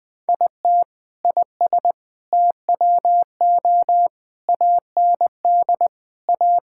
Morse code, heard as a keyed tone, about 20 wpm, 700 Hz. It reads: IT IS TWO AND A
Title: Sentences with 4-Word Phrases - Spoken Beginning - 20wpm